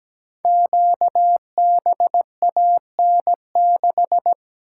Morse code, heard as a keyed tone, about 17 wpm, 700 Hz. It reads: QBAN6
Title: Code Groups - 5 Letter:Numbers - 17wpm